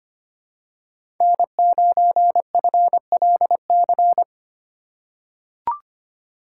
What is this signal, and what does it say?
Morse code, heard as a keyed tone, about 25 wpm, 700 Hz. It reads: N9FLC
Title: USA Call Signs - 5 Characters - 3x Character Spacing - 25wpm